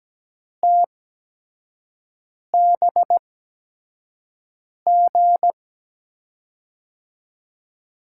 Morse code, heard as a keyed tone, about 17 wpm, spaced as for 5 wpm, 700 Hz. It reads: TBG
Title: USA Call Signs - 6 Characters - 8x Character Spacing - 17wpm